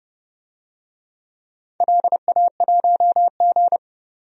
Morse code, heard as a keyed tone, about 30 wpm, 700 Hz. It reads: LA1G
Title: DXCC-100 Call Signs - 4 Characters - 30wpm